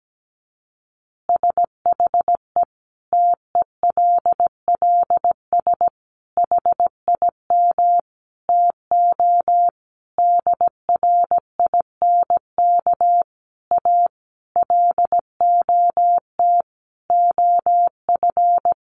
Morse code, heard as a keyed tone, about 17 wpm, 700 Hz. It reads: SHE TELLS HIM TO DRINK A LOT OF
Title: Additional Easy Phrases 17wpm